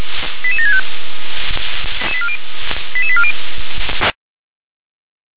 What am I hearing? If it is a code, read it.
Five-tone selective call (ZVEI), heard as quick runs of five tones, tones of about 70 ms. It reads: 88765 88840